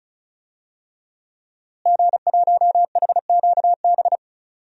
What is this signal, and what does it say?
Morse code, heard as a keyed tone, about 36 wpm, 700 Hz. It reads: G1HQB
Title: DXCC-100 Call Signs - 5 Characters - Repeated 3x - 35wpm